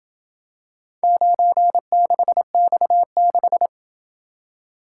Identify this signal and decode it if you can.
Morse code, heard as a keyed tone, about 26 wpm, 700 Hz. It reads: 96X6